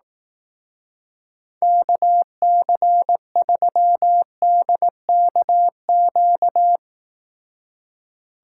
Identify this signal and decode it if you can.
Morse code, heard as a keyed tone, about 18 wpm, 700 Hz. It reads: KC3DKQ
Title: Speed Racing - USA 2x3 Call Signs 15wpm